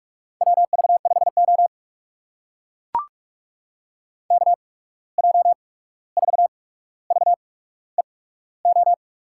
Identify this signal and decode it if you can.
Morse code, heard as a keyed tone, about 44 wpm, spaced as for 17 wpm, 700 Hz. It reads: WV5O KJ4VEO
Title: USA Call Signs - 8x Character Spacing - 45wpm